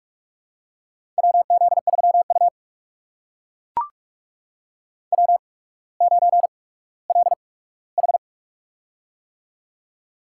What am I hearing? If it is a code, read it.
Morse code, heard as a keyed tone, about 44 wpm, spaced as for 19 wpm, 690 Hz. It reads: WZ3U W9LH
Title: USA Call Signs - 4 Characters - 8x Character Spacing - 45wpm